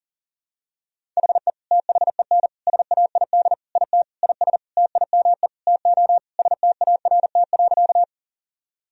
Morse code, heard as a keyed tone, about 40 wpm, 700 Hz. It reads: HE THEN SAID IT IS TIME TO START.
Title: Sentences from Top 1000 Words - Review Part 1 - 40wpm